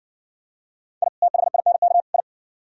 Morse code, heard as a keyed tone, about 58 wpm, 700 Hz. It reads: I THINK I